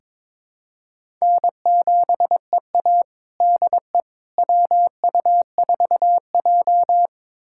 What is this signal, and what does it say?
Morse code, heard as a keyed tone, about 22 wpm, 700 Hz. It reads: N7EA DE WU4J
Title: USA Callsign DE Callsign - 4 Characters - 22wpm